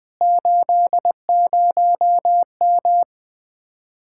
Morse code, heard as a keyed tone, about 20 wpm, 700 Hz. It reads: 80M
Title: Speed-Racing Common QSO Phrases - 20wpm